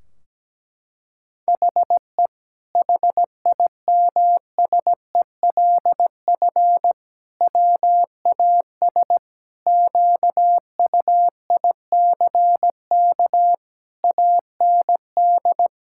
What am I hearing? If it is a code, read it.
Morse code, heard as a keyed tone, about 17 wpm, 700 Hz. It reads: HE HIMSELF WAS QUICK AND
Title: Sentences from Top 1600 Words 17wpm